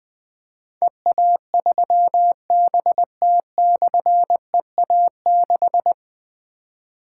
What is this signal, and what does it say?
Morse code, heard as a keyed tone, about 20 wpm, 700 Hz. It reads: EA3BT/EA6
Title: Difficult Call Signs 20wpm